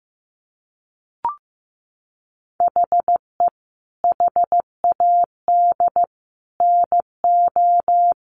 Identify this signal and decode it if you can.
Morse code, heard as a keyed tone, about 15 wpm, 700 Hz. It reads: HE HAD NO
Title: Sentences from Top 1400 Words 15wpm